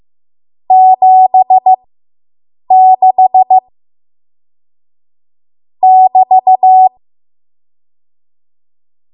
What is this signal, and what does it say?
Morse code, heard as a keyed tone, about 15 wpm, spaced as for 7 wpm, 750 Hz. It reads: 76 <BT>